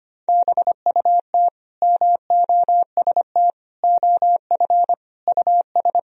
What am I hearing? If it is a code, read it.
Morse code, heard as a keyed tone, about 25 wpm, 700 Hz. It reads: BUT MOST OF US
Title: Sentences from Top 1000 Words - Review Part 1 - 25wpm